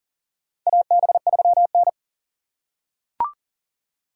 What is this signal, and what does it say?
Morse code, heard as a keyed tone, about 40 wpm, 700 Hz. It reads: AB3N
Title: USA Call Signs - 4 Characters - 6x Character Spacing - 40wpm